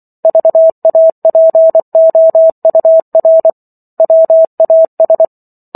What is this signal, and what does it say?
Morse code, heard as a keyed tone, about 24 wpm, 650 Hz. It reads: VAPOUR WAS